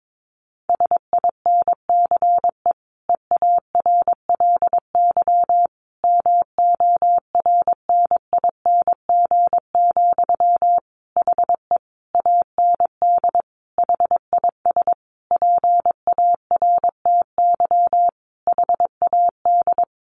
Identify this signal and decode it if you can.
Morse code, heard as a keyed tone, about 22 wpm, 700 Hz. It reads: SINCE EARLY MORNING, HE AND HIS PARTY HAD
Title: Sentences from Top 1400 Words 22wpm